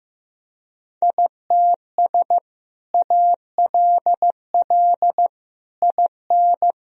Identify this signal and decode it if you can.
Morse code, heard as a keyed tone, about 16 wpm, 700 Hz. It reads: ITS ALL IN
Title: Sentences from Top 1000 Words - Review Part 1 - 15wpm